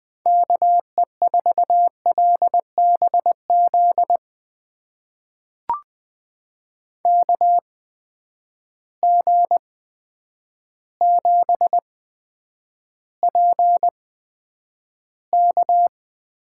Morse code, heard as a keyed tone, about 20 wpm, spaced as for 9 wpm, 700 Hz. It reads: KE4LBZ KG7PK
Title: USA Call Signs - 6 Characters - 8x Character Spacing - 20wpm